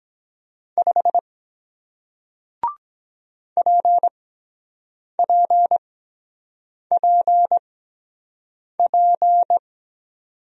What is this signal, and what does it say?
Morse code, heard as a keyed tone, about 22 wpm, spaced as for 7 wpm, 700 Hz. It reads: 5 PPPP